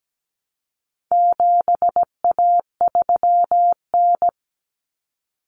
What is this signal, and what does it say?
Morse code, heard as a keyed tone, about 17 wpm, 700 Hz. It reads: 7A3N
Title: Common Cut Numbers - 4-Digit 17wpm